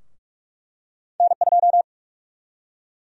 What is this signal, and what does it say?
Morse code, heard as a keyed tone, about 46 wpm, 700 Hz. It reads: N2
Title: USA Call Sign Prefixes - 45wpm